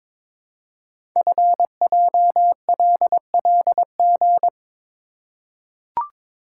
Morse code, heard as a keyed tone, about 22 wpm, 700 Hz. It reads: FJLLG
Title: Code Groups - 5 Letters - 22wpm